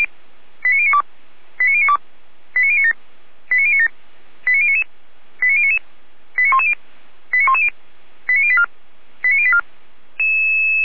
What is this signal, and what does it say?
Five-tone selective call (ZVEI), heard as quick runs of five tones, tones of about 70 ms. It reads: 89092 89092 89097 89097 89099 89099 89110 89110 89084 89084